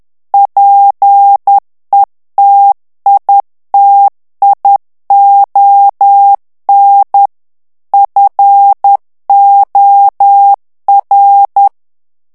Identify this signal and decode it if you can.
Morse code, heard as a keyed tone, about 10 wpm, 800 Hz. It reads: PETITION FOR
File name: ANGLAIS texte clair 3 vitesse 10 mots minute